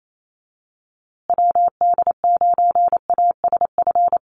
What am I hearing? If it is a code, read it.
Morse code, heard as a keyed tone, about 28 wpm, 700 Hz. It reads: WD9ASF
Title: USA Call Signs Repeated 2x 28wpm